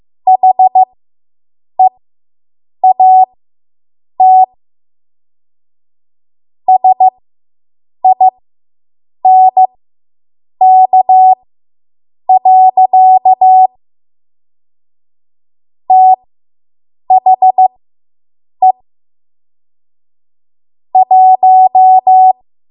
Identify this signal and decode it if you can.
Morse code, heard as a keyed tone, about 15 wpm, spaced as for 7 wpm, 750 Hz. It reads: HEAT SINK. THE 1